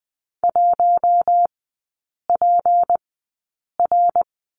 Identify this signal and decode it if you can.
Morse code, heard as a keyed tone, about 20 wpm, spaced as for 8 wpm, 690 Hz. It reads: 1PR
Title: 198 Lesson 32e - Letter k - Three Characters 20wpm